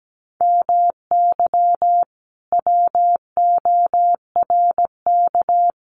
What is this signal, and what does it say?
Morse code, heard as a keyed tone, about 17 wpm, 700 Hz. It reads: MY WORK
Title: Sentences from Top 1300 Words 17wpm